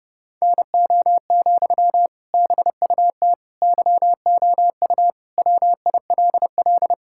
Morse code, heard as a keyed tone, about 31 wpm, 700 Hz. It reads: NO, BUT YOU WILL
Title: Sentences from Top 1000 Words - Review Part 6 - 30wpm